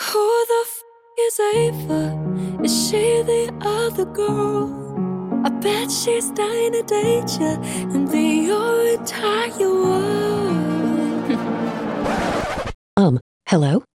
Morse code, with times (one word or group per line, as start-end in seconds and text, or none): none